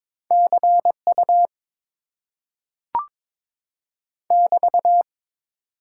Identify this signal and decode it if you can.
Morse code, heard as a keyed tone, about 22 wpm, 700 Hz. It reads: CU <BT>
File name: Top 90 QSO Elements - 8x Character Spacing - 22wpm